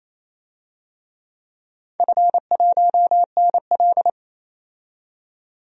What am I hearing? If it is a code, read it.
Morse code, heard as a keyed tone, about 28 wpm, 700 Hz. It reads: F1NL
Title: DXCC-100 Call Signs - 4 Characters - 28wpm